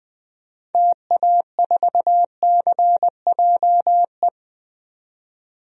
Morse code, heard as a keyed tone, about 20 wpm, 700 Hz. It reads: TA4CJE
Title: DXCC-100 Call Signs - 6 Characters - 20wpm